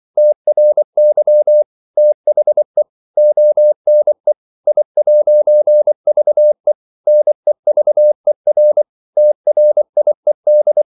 Morse code, heard as a keyed tone, about 24 wpm, 600 Hz. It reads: TRY THE ONE I'VE NEVER TRIED
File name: CW / Practice / EnglishQuotations0324